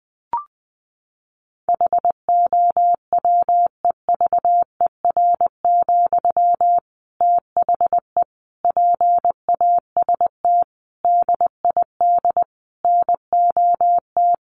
Morse code, heard as a keyed tone, about 20 wpm, 700 Hz. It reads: HOWEVER, THE PAST DID NOT
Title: Sentences from Top 1400 Words 20wpm